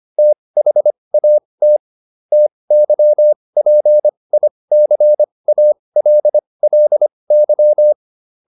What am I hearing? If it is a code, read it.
Morse code, heard as a keyed tone, about 25 wpm, spaced as for 20 wpm, 600 Hz. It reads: THAT TYPICALLY